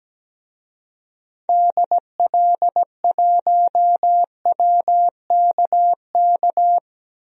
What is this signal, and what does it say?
Morse code, heard as a keyed tone, about 17 wpm, 700 Hz. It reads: DL1WKK
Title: DXCC-100 Call Signs - 6 Characters - 17wpm